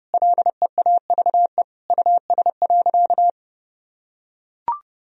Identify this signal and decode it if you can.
Morse code, heard as a keyed tone, about 30 wpm, 700 Hz. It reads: LEAVE US.
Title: Sentences from Top 1000 Words - Review Part 3 - 30wpm